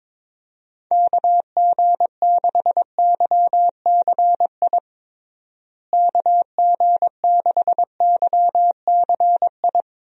Morse code, heard as a keyed tone, about 22 wpm, 700 Hz. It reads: KG6YCI KG6YCI
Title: USA Call Signs - 6 Characters - Repeated 3x - 22wpm